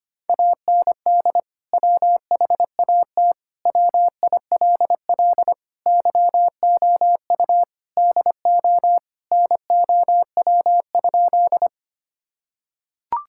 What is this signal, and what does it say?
Morse code, heard as a keyed tone, about 25 wpm, 700 Hz. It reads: AND WHAT WILL YOU DO NOW?
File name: practice / 25wpm / Sentences with 2-Word Phrases - 25wpm